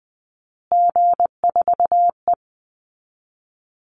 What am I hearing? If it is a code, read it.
Morse code, heard as a keyed tone, about 20 wpm, 700 Hz. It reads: G4E
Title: Call Signs Repeated 3x - 20wpm